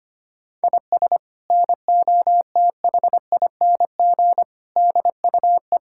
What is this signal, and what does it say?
Morse code, heard as a keyed tone, about 25 wpm, 700 Hz. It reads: IS NOTHING DUE
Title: Sentences from Top 1000 Words - Review Part 6 - 25wpm